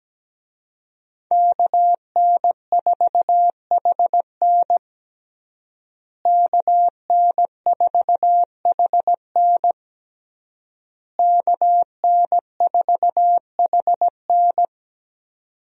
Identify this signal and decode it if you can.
Morse code, heard as a keyed tone, about 17 wpm, 700 Hz. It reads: KN4HN KN4HN KN4HN
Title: USA Call Signs - 5 Characters - Repeated 3x - 17wpm